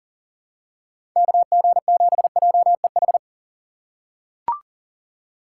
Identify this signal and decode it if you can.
Morse code, heard as a keyed tone, about 40 wpm, 700 Hz. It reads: KG7JEH
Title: USA Call Signs - 6 Characters - 4x Character Spacing - 40wpm